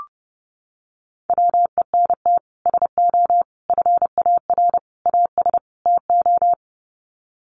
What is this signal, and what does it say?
Morse code, heard as a keyed tone, about 30 wpm, 700 Hz. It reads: WENT SO FAR AS TO
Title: Sets of 5 Words Encore 30wpm